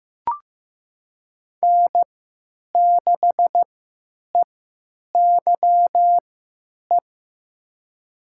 Morse code, heard as a keyed tone, about 15 wpm, spaced as for 9 wpm, 700 Hz. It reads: N6EYE
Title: USA Call Signs - 5 Characters - 3x Character Spacing - 15wpm